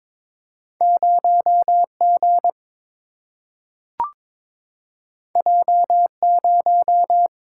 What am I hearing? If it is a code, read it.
Morse code, heard as a keyed tone, about 22 wpm, 700 Hz. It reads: 0 G J 0
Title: Single-Letter Single-Number 22wpm